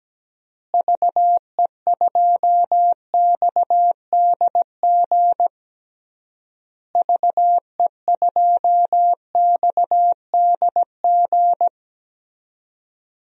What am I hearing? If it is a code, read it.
Morse code, heard as a keyed tone, about 17 wpm, 700 Hz. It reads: VE2XDG VE2XDG
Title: DXCC-100 Call Signs - 6 Characters - Repeated 3x - 17wpm